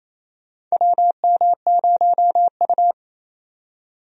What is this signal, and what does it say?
Morse code, heard as a keyed tone, about 28 wpm, 700 Hz. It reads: WM0U
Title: USA Call Signs - 4 Characters - 28wpm